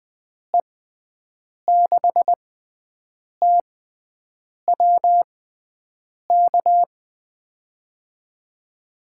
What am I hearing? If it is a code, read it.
Morse code, heard as a keyed tone, about 20 wpm, spaced as for 7 wpm, 700 Hz. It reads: E6TWK
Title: USA Call Signs - 6x Character Spacing - 20wpm